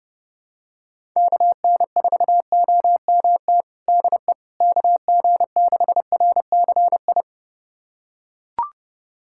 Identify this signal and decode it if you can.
Morse code, heard as a keyed tone, about 30 wpm, 700 Hz. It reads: KN4OMT DE KG6RCI
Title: USA Callsign DE Callsign - 6 Characters - 30wpm